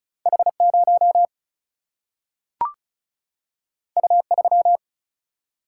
Morse code, heard as a keyed tone, about 34 wpm, 700 Hz. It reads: H0 U3
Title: Single-Letter Single-Number 35wpm